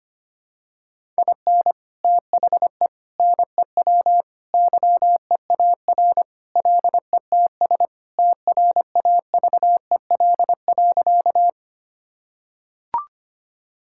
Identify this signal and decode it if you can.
Morse code, heard as a keyed tone, about 25 wpm, 700 Hz. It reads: IN THE NEW YEAR LETS TRAVEL.
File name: practice / 25wpm / Sentences with 3-Word Phrases - 25wpm